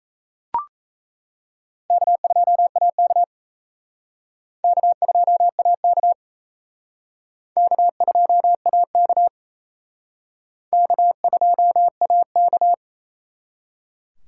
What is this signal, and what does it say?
Morse code, heard as a keyed tone, about 34 wpm, 700 Hz. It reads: K2AK K2AK K2AK K2AK
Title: Speed Racing - USA 1x2 Call Signs 28wpm